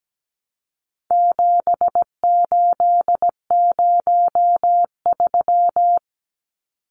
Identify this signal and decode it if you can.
Morse code, heard as a keyed tone, about 17 wpm, 700 Hz. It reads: 7803
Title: Numbers - 4-Digit 17wpm